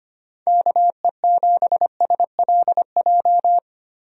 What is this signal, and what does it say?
Morse code, heard as a keyed tone, about 25 wpm, 700 Hz. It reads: KE7SLJ
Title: USA Call Signs - 6 Characters - Repeated 3x - 25wpm